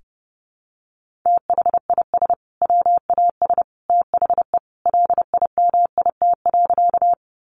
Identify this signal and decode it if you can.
Morse code, heard as a keyed tone, about 30 wpm, 700 Hz. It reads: THIS WAS THE LIMIT.
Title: Sentences from Top 1500 Words 30wpm